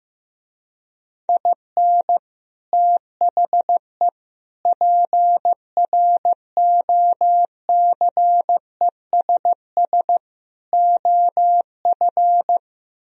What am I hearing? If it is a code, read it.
Morse code, heard as a keyed tone, about 15 wpm, 700 Hz. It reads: IN THE PROCESS OF
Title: Speed-Racing - Sets of 4 Words 15wpm